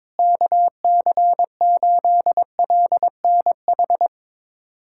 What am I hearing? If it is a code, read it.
Morse code, heard as a keyed tone, about 22 wpm, 700 Hz. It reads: KC8LNH
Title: USA Callsign DE Callsign 22wpm